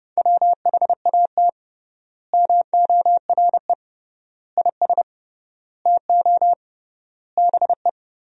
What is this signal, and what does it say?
Morse code, heard as a keyed tone, about 30 wpm, 700 Hz. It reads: WHAT MORE IS TO BE
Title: 3x Word-Spacing - Sentences from Top 500 Words - 30wpm